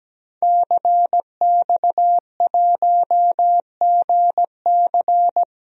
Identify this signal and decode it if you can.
Morse code, heard as a keyed tone, about 17 wpm, 700 Hz. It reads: CX1GC
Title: DXCC-100 Call Signs - 5 Characters - Repeated 3x - 17wpm